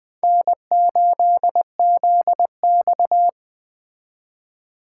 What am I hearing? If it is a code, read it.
Morse code, heard as a keyed tone, about 22 wpm, 700 Hz. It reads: N8ZX